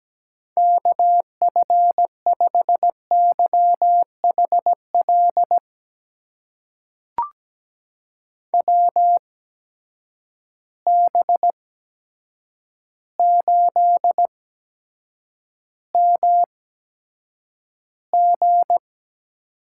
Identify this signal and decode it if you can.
Morse code, heard as a keyed tone, about 17 wpm, spaced as for 8 wpm, 700 Hz. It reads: KF5YHL WB8MG
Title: USA Call Signs - 6 Characters - 8x Character Spacing - 17wpm